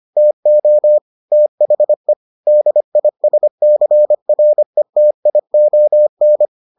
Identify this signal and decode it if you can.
Morse code, heard as a keyed tone, about 25 wpm, 600 Hz. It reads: TO THE DISCRETION